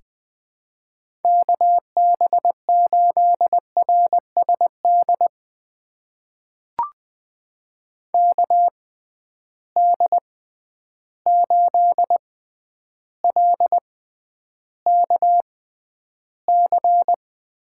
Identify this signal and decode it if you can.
Morse code, heard as a keyed tone, about 20 wpm, spaced as for 10 wpm, 700 Hz. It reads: KB8RSD KD8LKC
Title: USA Call Signs - 6 Characters - 6x Character Spacing - 20wpm